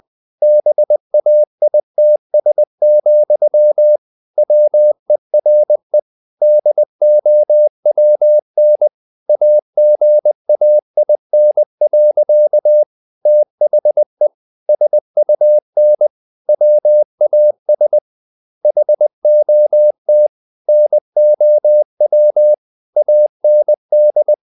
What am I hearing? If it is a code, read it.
Morse code, heard as a keyed tone, about 20 wpm, 600 Hz. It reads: BAITS, WERE DOWN AGAIN. THE SUN WAS HOT NOW AND
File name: the old man and the sea / the old man and the sea part 1 20wpm